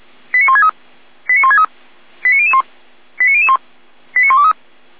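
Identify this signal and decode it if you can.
Five-tone selective call (ZVEI), heard as quick runs of five tones, tones of about 70 ms. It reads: 89163 89163 89001 89001 89123
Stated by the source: ZVEI five-tone selective call received via SDR.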